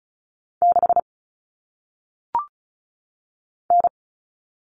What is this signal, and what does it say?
Morse code, heard as a keyed tone, about 36 wpm, 700 Hz. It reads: 6 N
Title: All Cut Numbers - 1-Digit 35wpm